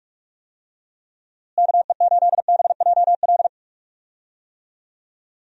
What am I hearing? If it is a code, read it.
Morse code, heard as a keyed tone, about 42 wpm, 700 Hz. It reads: KE8BJL
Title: USA Call Signs - 6 Characters - Repeated 2x - 45wpm